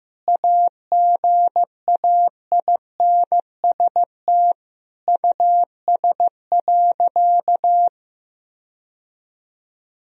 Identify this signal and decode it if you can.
Morse code, heard as a keyed tone, about 16 wpm, 700 Hz. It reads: AGAINST US.
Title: Sentences from Top 1000 Words - Review Part 2 - 15wpm